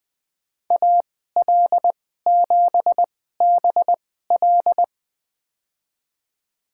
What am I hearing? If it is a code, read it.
Morse code, heard as a keyed tone, about 20 wpm, spaced as for 14 wpm, 700 Hz. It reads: AL7BL
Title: USA Call Signs - 5 Characters - 2x Character Spacing - 20wpm